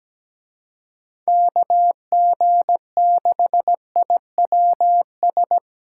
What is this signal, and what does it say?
Morse code, heard as a keyed tone, about 20 wpm, 700 Hz. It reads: KG6IWS
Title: Speed Racing - USA Call Signs 17wpm